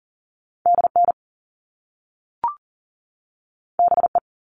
Morse code, heard as a keyed tone, about 40 wpm, 700 Hz. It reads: DN BE
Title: All Cut Numbers - 2-Digit 40wpm